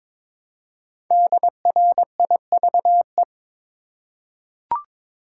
Morse code, heard as a keyed tone, about 22 wpm, 700 Hz. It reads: DRIVE
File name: 5 Letter Words - First 3 Letters Given - 22wpm